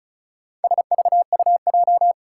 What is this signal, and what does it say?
Morse code, heard as a keyed tone, about 36 wpm, 700 Hz. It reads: SVUJ
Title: Four Challenging Letter-Number Combinations 35wpm